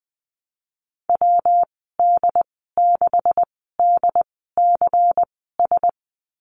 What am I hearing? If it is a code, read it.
Morse code, heard as a keyed tone, about 19 wpm, spaced as for 14 wpm, 700 Hz. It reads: WD6DCS
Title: 179 Lesson 29g - Slash - Call Signs 20wpm